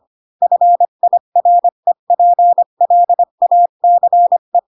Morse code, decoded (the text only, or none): FIREPLACE